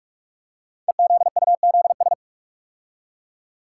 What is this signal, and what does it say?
Morse code, heard as a keyed tone, about 44 wpm, 690 Hz. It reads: EZUZS
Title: Code Groups - 5 Letters - 45wpm